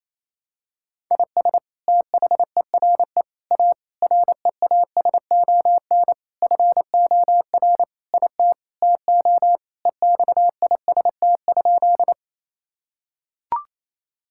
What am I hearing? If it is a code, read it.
Morse code, heard as a keyed tone, about 28 wpm, 700 Hz. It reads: IS THERE A REASON FOR IT TO EXIST?
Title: Sentences from Top 1000 Words - Review Part 5 - 28wpm